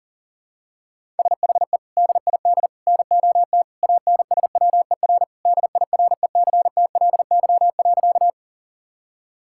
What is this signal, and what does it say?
Morse code, heard as a keyed tone, about 40 wpm, 700 Hz. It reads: SHE DID NOT ANSWER DIRECTLY.
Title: Sentences from Top 1000 Words - Review Part 5 - 40wpm